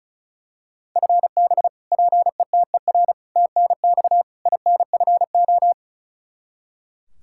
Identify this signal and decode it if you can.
Morse code, heard as a keyed tone, about 34 wpm, 700 Hz. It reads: FB PETER TNX INFO
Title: Practice QSO - Exchange 3 - More Info - 35wpm